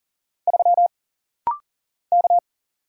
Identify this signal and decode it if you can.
Morse code, heard as a keyed tone, about 40 wpm, 700 Hz. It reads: 3 K
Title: Single Letter-Number Alt - Rapid-Fire 40wpm